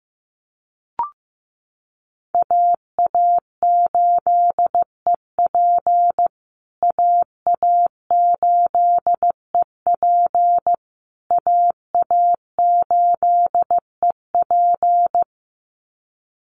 Morse code, heard as a keyed tone, about 15 wpm, 700 Hz. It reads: AA8EP AA8EP AA8EP
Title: USA Call Signs Repeated 3x 15wpm